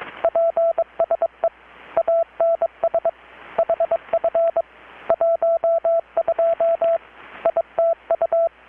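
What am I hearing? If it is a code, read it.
Morse code, heard as a keyed tone, about 21 wpm, 670 Hz, with radio noise. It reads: PSE ANS HF 12 ITU
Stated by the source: Morse from KPH's Night of Nights XX maritime radiotelegraphy event, received in Chiapas, Mexico about 3,800 km away on a Kenwood R-600 with a grounded 10 m long-wire antenna.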